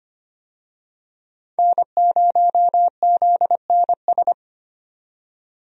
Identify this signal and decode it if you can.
Morse code, heard as a keyed tone, about 25 wpm, 700 Hz. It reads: N0ZNS
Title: USA Call Signs - 5 Characters - 25wpm